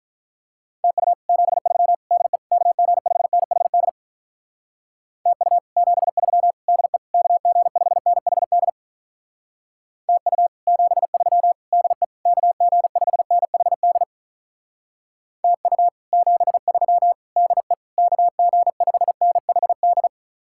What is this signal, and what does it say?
Morse code, heard as a keyed tone, about 45 wpm, 700 Hz. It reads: TU 73 DE KG5NHD TU 73 DE KG5NHD TU 73 DE KG5NHD TU 73 DE KG5NHD